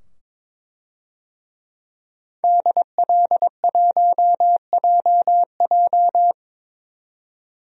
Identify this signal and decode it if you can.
Morse code, heard as a keyed tone, about 22 wpm, 700 Hz. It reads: DL1JJ
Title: DXCC-100 Call Signs - 5 Characters - 22wpm